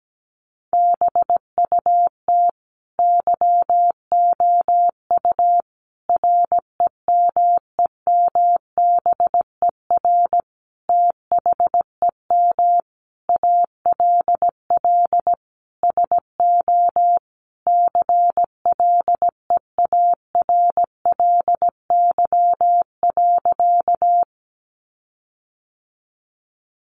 Morse code, heard as a keyed tone, about 17 wpm, 700 Hz. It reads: BUT YOU REMEMBER THEM ALL SO CLEARLY.
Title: Sentences from Top 1300 Words 17wpm